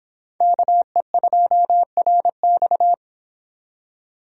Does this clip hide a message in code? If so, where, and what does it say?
Yes: Morse code from 0.4 to 2.9 s, heard as a keyed tone, about 25 wpm, 700 Hz. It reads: KE2RX